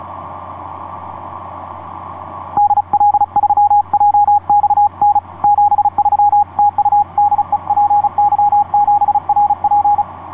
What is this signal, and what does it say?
Morse code, heard as a keyed tone, about 34 wpm, 810 Hz, with radio noise. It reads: DL3JXN 73 TU DE PY7RP
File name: py7rp dl3jxn etc 12m cw 07mar2010 1614